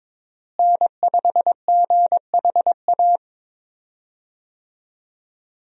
Morse code, heard as a keyed tone, about 22 wpm, 700 Hz. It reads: N5GHA
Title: USA Call Signs - 5 Characters - Repeated 2x - 22wpm